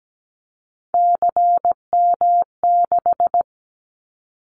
Morse code, heard as a keyed tone, about 17 wpm, 700 Hz. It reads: CM6
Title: Callsign Prefixes for Top 100 DXCC - 17wpm